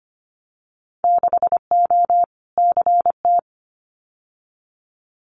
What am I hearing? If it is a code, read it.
Morse code, heard as a keyed tone, about 25 wpm, 700 Hz. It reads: 6O CT
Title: Winter Field Day Exchanges Encore 25wpm